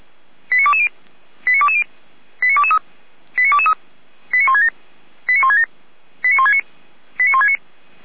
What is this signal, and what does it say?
Five-tone selective call (ZVEI), heard as quick runs of five tones, tones of about 70 ms. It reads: 89220 89220 89223 89223 89167 89167 89169 89169
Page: Server at audio.f85360.feuerwehr-orlamuende.org